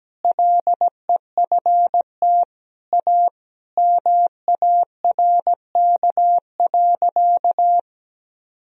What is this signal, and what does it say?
Morse code, heard as a keyed tone, about 17 wpm, 700 Hz. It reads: LEFT A MARK.
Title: Sentences from Top 1000 Words - Review Part 1 - 17wpm